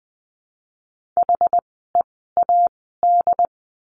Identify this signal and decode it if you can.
Morse code, heard as a keyed tone, about 20 wpm, spaced as for 15 wpm, 700 Hz. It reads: HEAD
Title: 101 Lesson 17f - Letter u - Words 20wpm